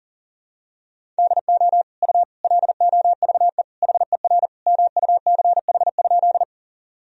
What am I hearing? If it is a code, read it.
Morse code, heard as a keyed tone, about 40 wpm, 700 Hz. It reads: DO U LOVE HER MUCH?
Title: Sentences from Top 400 Words and Abbreviations 40wpm